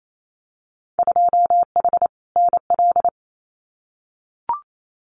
Morse code, heard as a keyed tone, about 28 wpm, 700 Hz. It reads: 2H NL
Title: Winter Field Day Exchanges Encore 28wpm